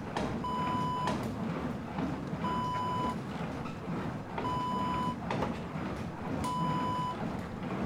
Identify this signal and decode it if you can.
There is no code.